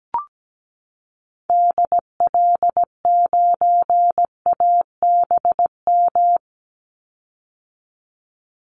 Morse code, heard as a keyed tone, about 17 wpm, 700 Hz. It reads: DL9ABM
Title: Call Signs - 17wpm